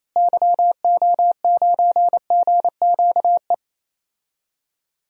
Morse code, heard as a keyed tone, about 28 wpm, 700 Hz. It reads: YO9GQE
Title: DXCC-100 Call Signs - 6 Characters - Repeated 2x - 28wpm